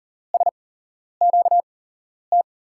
Morse code, heard as a keyed tone, about 41 wpm, spaced as for 11 wpm, 700 Hz. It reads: SQT
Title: USA Call Signs - 6 Characters - 8x Character Spacing - 40wpm